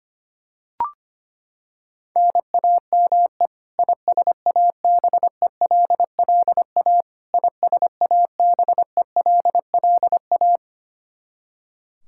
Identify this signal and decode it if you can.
Morse code, heard as a keyed tone, about 25 wpm, 700 Hz. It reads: NAME ISABELLA ISABELLA
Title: Practice QSO - Exchange 1 - Beginning - 25wpm